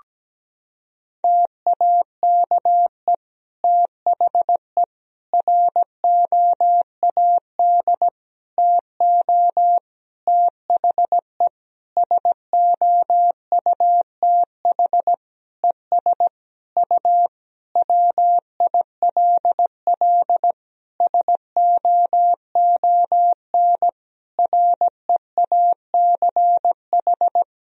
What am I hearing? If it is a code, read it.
Morse code, heard as a keyed tone, about 17 wpm, 700 Hz. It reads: TAKE THE ROAD TO THE SOUTH ES U WILL SOON REACH
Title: Sentences from Top 400 Words and Abbreviations 17wpm